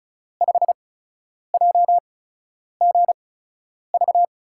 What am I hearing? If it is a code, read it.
Morse code, heard as a keyed tone, about 36 wpm, spaced as for 10 wpm, 700 Hz. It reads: 5JGV
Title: USA Call Signs - 5 Characters - 8x Character Spacing - 35wpm